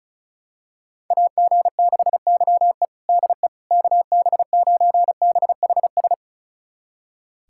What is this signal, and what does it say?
Morse code, heard as a keyed tone, about 36 wpm, 700 Hz. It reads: AG6YE DE KB9BHS
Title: USA Callsign DE Callsign 35wpm